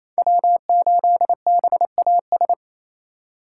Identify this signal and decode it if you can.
Morse code, heard as a keyed tone, about 27 wpm, 700 Hz. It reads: W8BAS
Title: USA Call Signs - 5 Characters - 2x Character Spacing - 28wpm